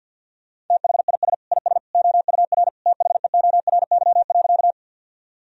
Morse code, heard as a keyed tone, about 50 wpm, 700 Hz. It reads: THIS IS OUR THEORY.
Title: Sentences from Top 1000 Words - Review Part 3 - 50wpm